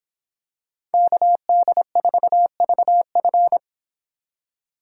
Morse code, heard as a keyed tone, about 25 wpm, 700 Hz. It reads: KD4VF